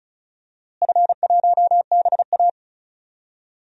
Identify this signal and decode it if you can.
Morse code, heard as a keyed tone, about 34 wpm, 700 Hz. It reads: F1BA
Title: DXCC-100 Call Signs - 4 Characters - Repeated 2x - 35wpm